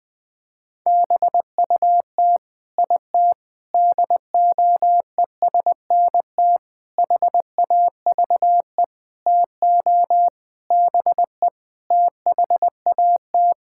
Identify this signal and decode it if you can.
Morse code, heard as a keyed tone, about 20 wpm, 700 Hz. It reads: BUT IT DOESNT HAVE TO BE THAT
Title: Sets of 8 Words - 20wpm